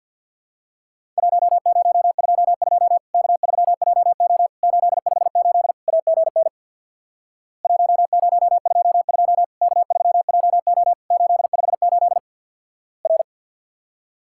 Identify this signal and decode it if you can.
Morse code, heard as a keyed tone, about 50 wpm, 700 Hz. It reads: 1022 K3JO 858 AGN 1022 K3JO 858 R
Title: Worked All Europe DX Contest - QTC Exchange with Repeats - 50wpm